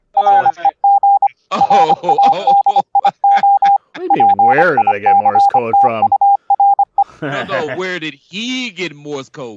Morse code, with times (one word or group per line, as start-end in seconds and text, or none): 0.2-1.3 REG
1.6-3.8 RADIO
4.1-7.0 PICTURE